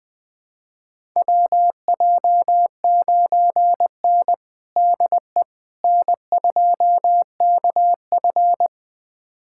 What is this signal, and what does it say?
Morse code, heard as a keyed tone, about 21 wpm, 700 Hz. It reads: WJ9N DE N2KF
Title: USA Callsign DE Callsign - 4 Characters - 20wpm